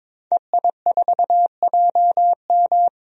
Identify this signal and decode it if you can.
Morse code, heard as a keyed tone, about 22 wpm, 700 Hz. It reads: EI4JM
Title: DXCC-100 Call Signs - 5 Characters - 22wpm